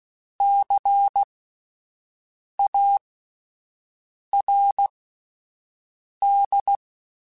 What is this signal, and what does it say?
Morse code, heard as a keyed tone, about 16 wpm, spaced as for 5 wpm, 800 Hz. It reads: CARD
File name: Spaced Alphas / Spaced Alphas mp3s / Cs sp @10